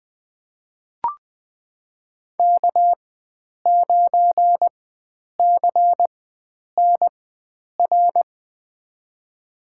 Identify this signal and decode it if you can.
Morse code, heard as a keyed tone, about 20 wpm, spaced as for 9 wpm, 700 Hz. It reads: K9CNR
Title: USA Call Signs - 5 Characters - 4x Character Spacing - 20wpm